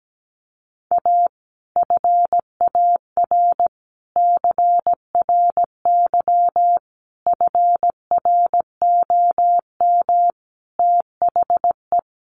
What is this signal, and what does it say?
Morse code, heard as a keyed tone, about 17 wpm, 700 Hz. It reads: A FAR CRY FROM THE
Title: Sets of 5 Words Encore 17wpm